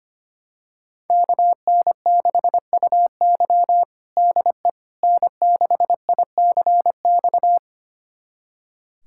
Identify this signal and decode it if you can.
Morse code, heard as a keyed tone, about 25 wpm, 700 Hz. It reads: KN6UY DE N6ICX
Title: USA Callsign DE Callsign - 5 Characters - 25wpm